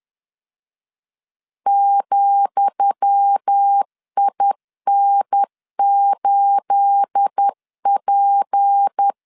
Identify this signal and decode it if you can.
Morse code, heard as a keyed tone, about 10 wpm, 800 Hz. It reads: ,IN8P
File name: LEZIONI / 10 SIGNS / 10D COMA 10wpm 3013